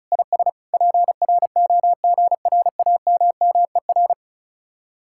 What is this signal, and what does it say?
Morse code, heard as a keyed tone, about 33 wpm, 700 Hz. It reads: IS PROGRAMMER
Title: Speed-Racing Common QSO Phrases - 35wpm